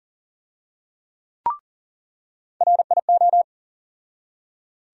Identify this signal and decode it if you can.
Morse code, heard as a keyed tone, about 39 wpm, 700 Hz. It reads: RIO
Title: USA Call Sign Postfixes - 3 Letters 40wpm